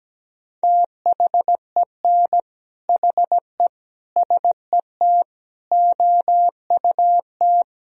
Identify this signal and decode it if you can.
Morse code, heard as a keyed tone, about 17 wpm, 700 Hz. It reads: THEN HE SET OUT
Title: Sentences from Top 1000 Words - Review Part 1 - 17wpm